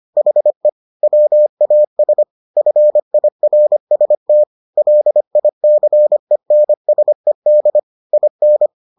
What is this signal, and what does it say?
Morse code, heard as a keyed tone, about 25 wpm, 600 Hz. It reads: HE WAS FIRST LICENSED IN